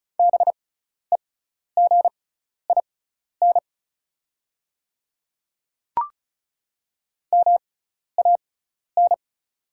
Morse code, heard as a keyed tone, about 34 wpm, spaced as for 12 wpm, 690 Hz. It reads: BEGIN MAN
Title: 6x Character Spacing - Top 300 Words - 35wpm